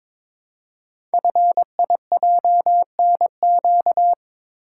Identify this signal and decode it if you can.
Morse code, heard as a keyed tone, about 22 wpm, 700 Hz. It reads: FIJNQ
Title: Code Groups - 5 Letters - 22wpm